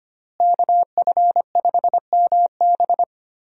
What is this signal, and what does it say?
Morse code, heard as a keyed tone, about 25 wpm, 700 Hz. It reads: KF5MB
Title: USA Call Signs - 5 Characters - Repeated 2x - 25wpm